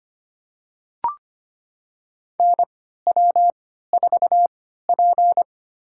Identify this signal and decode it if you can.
Morse code, heard as a keyed tone, about 25 wpm, spaced as for 14 wpm, 700 Hz. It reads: NW4P
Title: USA Call Signs - 4 Characters - 3x Character Spacing - 25wpm